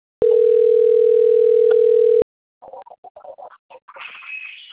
Ringback tone (North America), 440 + 480 Hz.